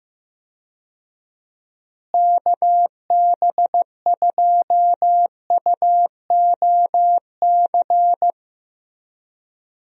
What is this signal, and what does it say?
Morse code, heard as a keyed tone, about 15 wpm, 700 Hz. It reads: KB2UOC
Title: USA Call Signs - 6 Characters - Repeated 2x - 15wpm